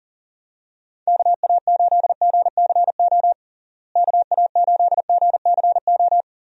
Morse code, heard as a keyed tone, about 40 wpm, 700 Hz. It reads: KA8GCO KA8GCO
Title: USA Call Signs - 6 Characters - Repeated 2x - 40wpm